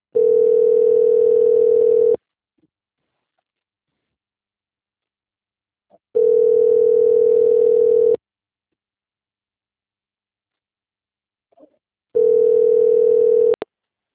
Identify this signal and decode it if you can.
Ringback tone (North America), 440 + 480 Hz.